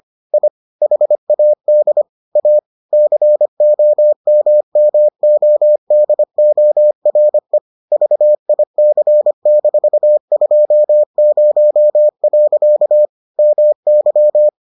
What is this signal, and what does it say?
Morse code, heard as a keyed tone, about 25 wpm, 600 Hz. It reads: I HAD A COMMODORE VIC-20. MY